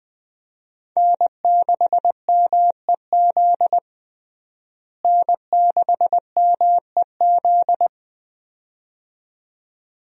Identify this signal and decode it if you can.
Morse code, heard as a keyed tone, about 21 wpm, 700 Hz. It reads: N6MEZ N6MEZ
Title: USA Call Signs - 5 Characters - Repeated 2x - 20wpm